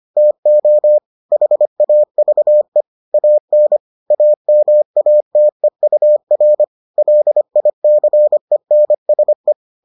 Morse code, heard as a keyed tone, about 25 wpm, 600 Hz. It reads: TO HAVE AN AMATEUR LICENSE